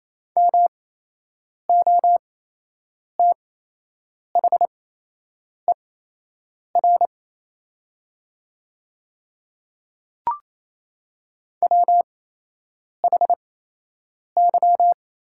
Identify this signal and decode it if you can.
Morse code, heard as a keyed tone, about 28 wpm, spaced as for 8 wpm, 700 Hz. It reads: MOTHER WHY
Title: 8x Character Spacing - Top 200 Words - 28wpm